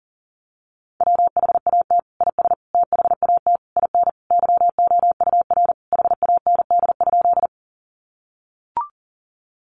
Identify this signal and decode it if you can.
Morse code, heard as a keyed tone, about 40 wpm, 700 Hz. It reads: WHAT IS THAT IN YOUR HAND?
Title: Sentences from Top 200 Words 40wpm